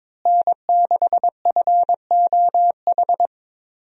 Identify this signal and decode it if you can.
Morse code, heard as a keyed tone, about 21 wpm, 700 Hz. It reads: N6FOH
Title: USA Call Signs - 5 Characters - Repeated 3x - 22wpm